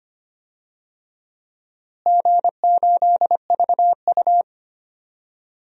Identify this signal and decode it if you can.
Morse code, heard as a keyed tone, about 25 wpm, 700 Hz. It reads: G8VU
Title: DXCC-100 Call Signs - 4 Characters - Repeated 2x - 25wpm